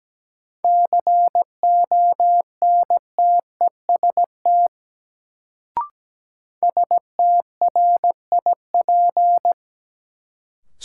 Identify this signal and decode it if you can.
Morse code, heard as a keyed tone, about 17 wpm, 700 Hz. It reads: CONTEST STRIP
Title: Sentences from Top 1600 Words - New Word Drill - 17wpm